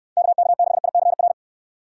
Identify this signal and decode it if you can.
Morse code, heard as a keyed tone, about 69 wpm, 700 Hz. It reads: KK6ICR